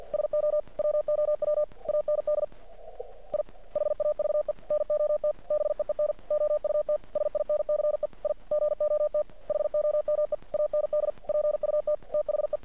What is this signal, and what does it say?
Morse code, heard as a keyed tone, about 50 wpm, 600 Hz. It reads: SO WOW AND I HAVE NOT BEEN OUT SINCE I GOT HOME AND PUT THE
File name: archived / CRSnet